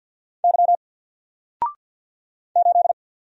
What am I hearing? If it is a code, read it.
Morse code, heard as a keyed tone, about 48 wpm, 700 Hz. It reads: Y 8
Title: Single Character (Letter:Number:Punctuation:Prosign) Alt 50wpm